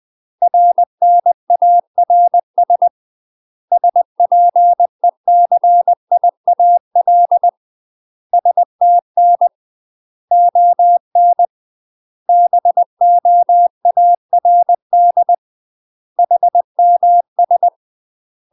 Morse code, heard as a keyed tone, about 20 wpm, 700 Hz. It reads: RNARS SPECIAL STN ON BOARD HMS
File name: UK MASTER QSOs TRAINING / 20WPM / UK MASTER QSOs 0026